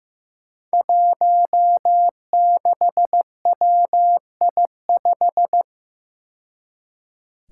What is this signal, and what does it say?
Morse code, heard as a keyed tone, about 15 wpm, 700 Hz. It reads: 16WI5
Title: Five Challenging Letter-Number Combinations 15wpm